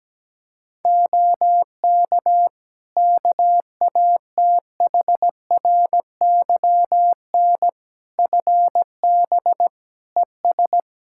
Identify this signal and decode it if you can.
Morse code, heard as a keyed tone, about 17 wpm, 700 Hz. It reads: OK KATHRYN FB ES
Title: K1USN SST QSO Practice 17wpm